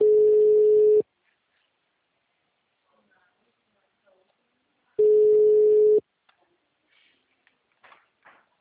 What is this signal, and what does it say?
Ringback tone (Europe, 425 Hz), with speech.